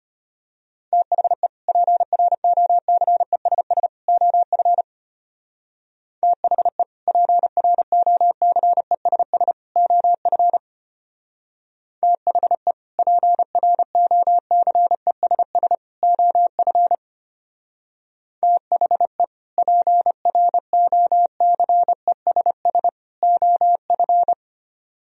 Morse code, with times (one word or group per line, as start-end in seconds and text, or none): 0.9-1.5 THE
1.7-3.9 PROCESS
4.1-4.8 OF
6.2-6.8 THE
7.1-9.5 PROCESS
9.8-10.6 OF
12.0-12.7 THE
13.0-15.8 PROCESS
16.0-16.9 OF
18.4-19.2 THE
19.6-22.9 PROCESS
23.2-24.3 OF